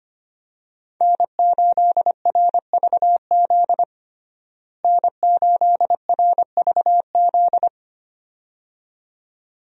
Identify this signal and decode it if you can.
Morse code, heard as a keyed tone, about 25 wpm, 700 Hz. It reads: N8RVZ N8RVZ
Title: USA Call Signs - 5 Characters - Repeated 2x - 25wpm